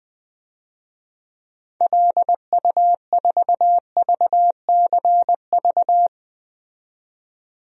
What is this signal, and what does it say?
Morse code, heard as a keyed tone, about 20 wpm, 700 Hz. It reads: LU4VCV
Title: DXCC-100 Call Signs - 6 Characters - 20wpm